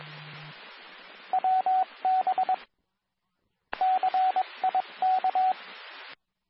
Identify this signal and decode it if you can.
Morse code, heard as a keyed tone, about 22 wpm, 750 Hz, with radio noise. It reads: WB CIK
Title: Nassau and Suffolk County Amateur Repeaters